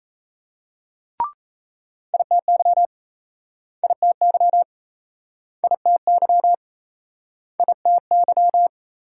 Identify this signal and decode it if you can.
Morse code, heard as a keyed tone, about 39 wpm, 700 Hz. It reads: ITY ITY ITY ITY